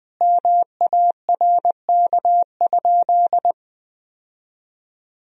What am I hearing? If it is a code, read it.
Morse code, heard as a keyed tone, about 21 wpm, 700 Hz. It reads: MARK?
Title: Sentences from Top 1000 Words - Review Part 1 - 20wpm